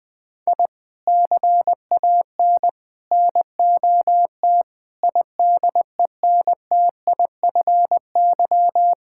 Morse code, heard as a keyed tone, about 20 wpm, 700 Hz. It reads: I CAN NOT IDENTIFY
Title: Sentences from Top 1000 Words - Review Part 6 - 20wpm